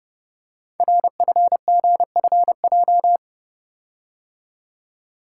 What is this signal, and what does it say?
Morse code, heard as a keyed tone, about 30 wpm, 700 Hz. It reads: RFGFJ
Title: Code Groups - 5 Letters - 30wpm